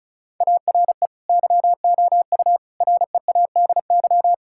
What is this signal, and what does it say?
Morse code, heard as a keyed tone, about 36 wpm, 700 Hz. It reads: ARE YOU READY
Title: Sentences from Top 1000 Words - Review Part 6 - 35wpm